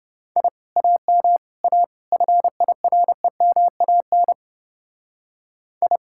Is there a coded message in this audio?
Yes: Morse code from 0.4 to 6.0 s, heard as a keyed tone, about 29 wpm, 700 Hz. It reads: I AM A FIREMAN I